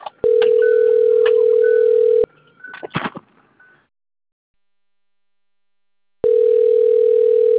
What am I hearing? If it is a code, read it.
Ringback tone (North America), 440 + 480 Hz.